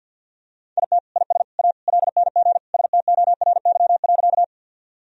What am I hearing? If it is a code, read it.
Morse code, heard as a keyed tone, about 51 wpm, 700 Hz. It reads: IT IS A LNG STORY.
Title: Sentences from Top 300 Words and Abbreviations 50wpm